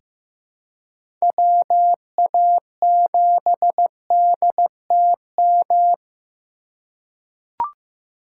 Morse code, heard as a keyed tone, about 15 wpm, 700 Hz. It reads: WA7DTM
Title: USA Call Signs - 4x Character Spacing - 15wpm